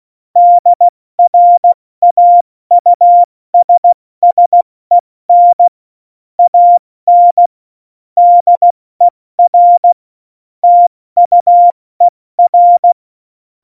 Morse code, heard as a keyed tone, about 16 wpm, 700 Hz. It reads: DRAUSSEN AN DER TUER
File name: Morsekurs / Witze / Witz 27 March 2018, 70 F80